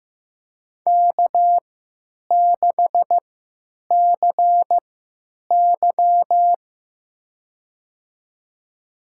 Morse code, heard as a keyed tone, about 15 wpm, spaced as for 9 wpm, 700 Hz. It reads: K6CY
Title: USA Call Signs - 4 Characters - 3x Character Spacing - 15wpm